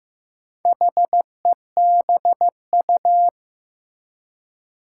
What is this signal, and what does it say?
Morse code, heard as a keyed tone, about 15 wpm, 700 Hz. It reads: HEBU
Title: Four Challenging Letter-Number Combinations 15wpm